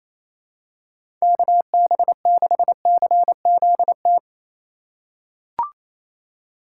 Morse code, heard as a keyed tone, about 28 wpm, 700 Hz. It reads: KB6CZT
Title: USA Call Signs - 6 Characters - 3x Character Spacing - 28wpm